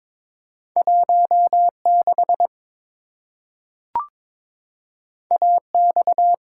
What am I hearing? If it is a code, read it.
Morse code, heard as a keyed tone, about 22 wpm, 700 Hz. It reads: 16 AX
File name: practice / 22wpm / Two Letter-Number Combinations - 22wpm